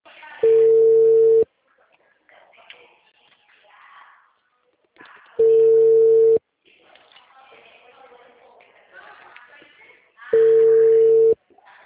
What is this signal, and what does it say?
Ringback tone (China, 450 Hz), with speech.
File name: out- -2009